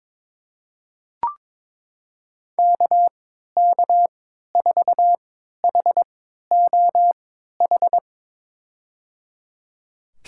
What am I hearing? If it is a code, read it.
Morse code, heard as a keyed tone, about 22 wpm, spaced as for 12 wpm, 700 Hz. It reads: KK4HOH
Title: USA Call Signs - 6 Characters - 3x Character Spacing - 22wpm